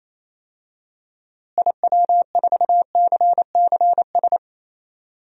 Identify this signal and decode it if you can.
Morse code, heard as a keyed tone, about 28 wpm, 700 Hz. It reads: IW4CCS
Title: DXCC-100 Call Signs - 6 Characters - Repeated 3x - 28wpm